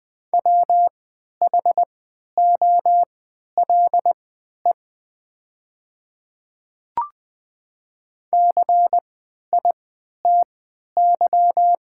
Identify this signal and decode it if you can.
Morse code, heard as a keyed tone, about 20 wpm, spaced as for 11 wpm, 700 Hz. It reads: WHOLE CITY
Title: 3x Character Spacing - Top 500 Words - 20wpm